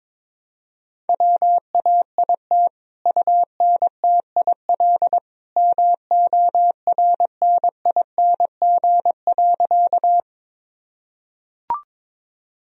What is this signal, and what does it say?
Morse code, heard as a keyed tone, about 22 wpm, 700 Hz. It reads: WAIT UNTIL MORNING.
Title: Sentences from Top 1000 Words - Review Part 1 - 22wpm